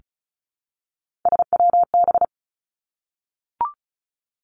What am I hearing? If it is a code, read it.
Morse code, heard as a keyed tone, about 36 wpm, 700 Hz. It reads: SWB
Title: Three Challenging Letter-Number Combinations 35wpm